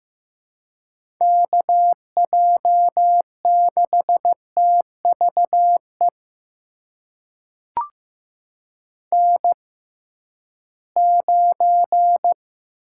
Morse code, heard as a keyed tone, about 15 wpm, spaced as for 11 wpm, 700 Hz. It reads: KJ6TVE N9
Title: USA Call Signs - 6x Character Spacing - 15wpm